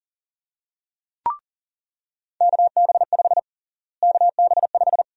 Morse code, heard as a keyed tone, about 40 wpm, 700 Hz. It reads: KB5 KB5
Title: USA Call Sign Prefixes - Repeated 2x - 40wpm